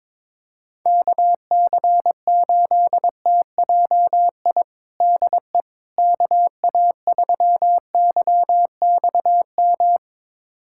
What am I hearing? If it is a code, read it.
Morse code, heard as a keyed tone, about 22 wpm, 700 Hz. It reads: KC8TJI DE KA3YXM
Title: USA Callsign DE Callsign - 6 Characters - 22wpm